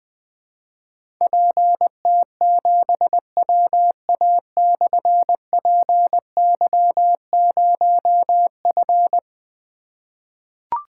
Morse code, heard as a keyed tone, about 20 wpm, 700 Hz. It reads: PT7WA/PY0F
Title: Difficult Call Signs 20wpm